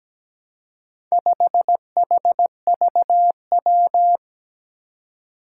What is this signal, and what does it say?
Morse code, heard as a keyed tone, about 17 wpm, 700 Hz. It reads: 5HVW
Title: Four Challenging Letter-Number Combinations 17wpm